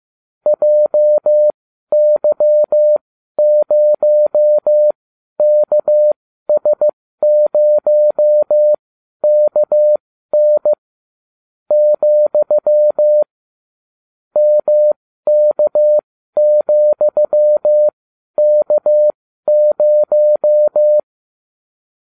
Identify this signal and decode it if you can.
Morse code, heard as a keyed tone, about 15 wpm, spaced as for 12 wpm, 600 Hz. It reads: JY0KS0KN , MK,K0